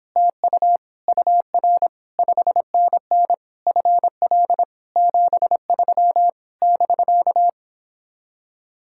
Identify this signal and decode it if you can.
Morse code, heard as a keyed tone, about 29 wpm, 700 Hz. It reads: TU UR 5NN FL 73 <BK>